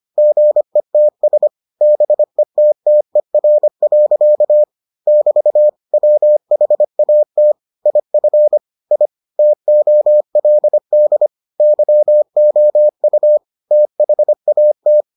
Morse code, heard as a keyed tone, about 25 wpm, 600 Hz. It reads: GETS BETTER. <BT> WHAT IF I TOLD YOU THAT